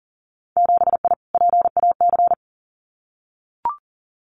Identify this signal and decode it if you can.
Morse code, heard as a keyed tone, about 40 wpm, 700 Hz. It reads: 7I PAC
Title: Winter Field Day Exchanges Encore 40wpm